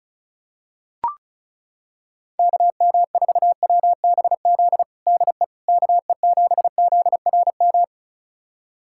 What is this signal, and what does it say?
Morse code, heard as a keyed tone, about 36 wpm, 700 Hz. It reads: KM4WBZ DE KE7ZRM
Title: USA Callsign DE Callsign 35wpm